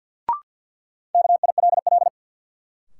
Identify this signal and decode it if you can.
Morse code, heard as a keyed tone, about 52 wpm, 700 Hz. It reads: KILL
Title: Sentences from Top 1400 Words - New Word Drill - 50wpm